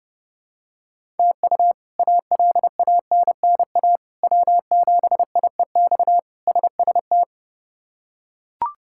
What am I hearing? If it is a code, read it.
Morse code, heard as a keyed tone, about 30 wpm, 700 Hz. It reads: TU ALANNA W7IEX SST
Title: K1USN SST QSO Practice - 30wpm